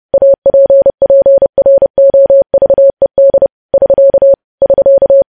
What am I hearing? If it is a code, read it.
Morse code, heard as a keyed tone, about 30 wpm, 550 Hz. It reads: APPROVED <SK> <SK>
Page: news headlines in morse code at 30 wpm